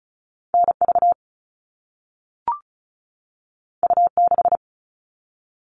Morse code, heard as a keyed tone, about 36 wpm, 700 Hz. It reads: NV U6
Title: All Cut Numbers - 2-Digit 35wpm